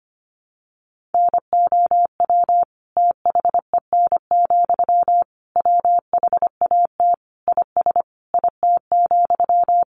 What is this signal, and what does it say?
Morse code, heard as a keyed tone, about 25 wpm, 700 Hz. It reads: NOW THEN, WHAT IS IT,
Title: Sentences from Top 1100 Words 25wpm